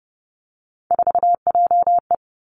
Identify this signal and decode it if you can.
Morse code, heard as a keyed tone, about 30 wpm, 700 Hz. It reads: 4JE
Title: Three Challenging Letter-Number Combinations 30wpm